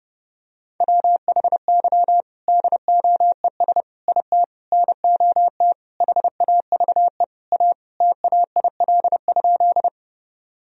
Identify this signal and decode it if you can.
Morse code, heard as a keyed tone, about 30 wpm, 700 Hz. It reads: WHY DOES IT NOT HAVE A TAIL?
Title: Sentences from Top 1000 Words - Review Part 3 - 30wpm